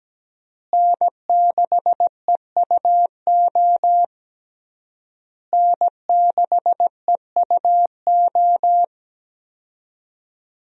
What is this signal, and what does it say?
Morse code, heard as a keyed tone, about 17 wpm, 700 Hz. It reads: N6EUO N6EUO
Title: USA Call Signs - 5 Characters - Repeated 3x - 17wpm